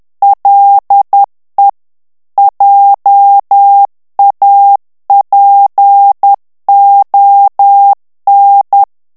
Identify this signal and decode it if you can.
Morse code, heard as a keyed tone, about 10 wpm, 800 Hz. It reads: LE JAPON
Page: Français - texte en clair 1 - vitesse : 10 mots minute